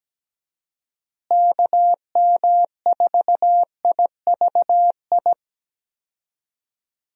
Morse code, heard as a keyed tone, about 17 wpm, 700 Hz. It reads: KM4IVI
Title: USA Call Signs - 6 Characters - Repeated 2x - 17wpm